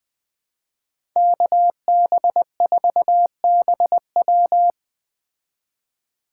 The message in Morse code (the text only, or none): KB4BW